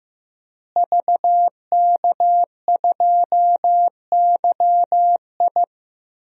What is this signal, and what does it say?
Morse code, heard as a keyed tone, about 15 wpm, 700 Hz. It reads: VK2YI
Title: DXCC-100 Call Signs - 5 Characters - Repeated 2x - 15wpm